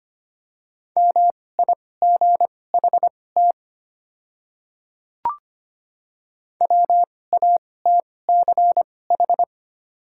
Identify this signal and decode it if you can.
Morse code, heard as a keyed tone, about 25 wpm, spaced as for 18 wpm, 700 Hz. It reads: MIGHT WATCH
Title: 2x Character Spacing - Top 500 Words - 25wpm